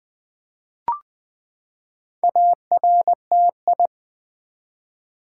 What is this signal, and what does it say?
Morse code, heard as a keyed tone, about 20 wpm, 700 Hz. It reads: ARTI
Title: Build A Word - 4 Letters At A Time - 8-Letter Words - 20wpm